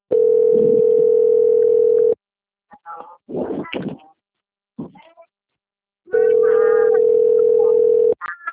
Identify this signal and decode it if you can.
Ringback tone (North America), 440 + 480 Hz.
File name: out- - - - - .60652